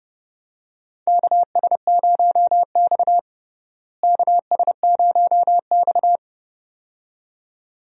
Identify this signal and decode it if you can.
Morse code, heard as a keyed tone, about 32 wpm, 700 Hz. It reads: KS0X KS0X
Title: USA Call Signs - 4 Characters - Repeated 2x - 30wpm